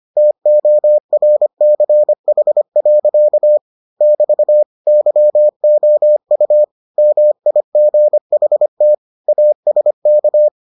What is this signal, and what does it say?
Morse code, heard as a keyed tone, about 25 wpm, 600 Hz. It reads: TORCH. <BT> YOU MIGHT ASK